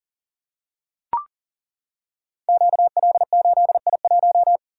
Morse code, heard as a keyed tone, about 39 wpm, 690 Hz. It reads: QL8I1
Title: Code Groups - 5 Letter:Numbers - 40wpm